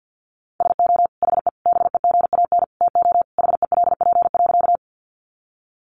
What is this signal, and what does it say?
Morse code, heard as a keyed tone, about 50 wpm, 700 Hz. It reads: SO HE BEGAN TO HELP.
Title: Sentences from Top 300 Words 50wpm